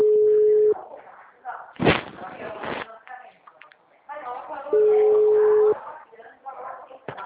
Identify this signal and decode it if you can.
Ringback tone (Europe, 425 Hz), with speech.